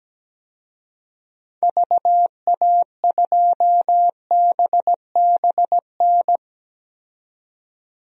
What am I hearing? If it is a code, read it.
Morse code, heard as a keyed tone, about 17 wpm, 700 Hz. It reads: VA2BBN
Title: DXCC-100 Call Signs - 6 Characters - 17wpm